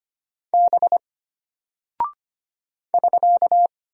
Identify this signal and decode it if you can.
Morse code, heard as a keyed tone, about 25 wpm, 700 Hz. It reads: B <SK>
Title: Single Character (Letter:Number:Punctuation:Prosign) Alt 25wpm